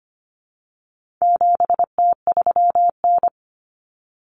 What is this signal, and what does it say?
Morse code, heard as a keyed tone, about 25 wpm, 700 Hz. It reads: 7T3N
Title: Common Cut Numbers - 4-Digit 25wpm